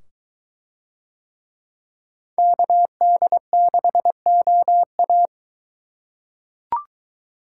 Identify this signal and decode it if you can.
Morse code, heard as a keyed tone, about 23 wpm, 700 Hz. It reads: KD6OA